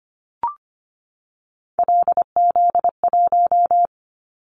Morse code, heard as a keyed tone, about 25 wpm, 700 Hz. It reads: LZ1
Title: Callsign Prefixes for Top 25 DXCC - 25wpm